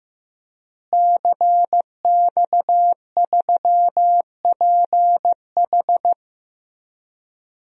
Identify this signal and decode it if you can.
Morse code, heard as a keyed tone, about 15 wpm, 700 Hz. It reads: CX3PH
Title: DXCC-100 Call Signs - 5 Characters - Repeated 3x - 15wpm